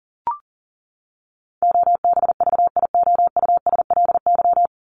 Morse code, heard as a keyed tone, about 40 wpm, 700 Hz. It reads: OBVIOUSLY
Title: Top 1501-1750 Words 40wpm